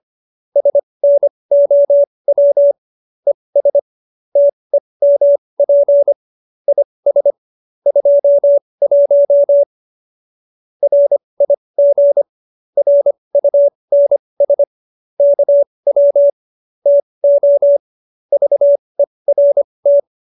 Morse code, heard as a keyed tone, about 25 wpm, spaced as for 20 wpm, 570 Hz. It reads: SNOW ES TEMP IS 21 RIG RUNS KW TO VERT
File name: QSO321 20